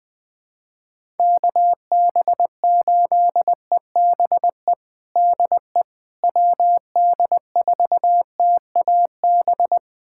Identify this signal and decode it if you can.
Morse code, heard as a keyed tone, about 20 wpm, 700 Hz. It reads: KB8EBE DE WD4TAB
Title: USA Callsign DE Callsign - 6 Characters - 20wpm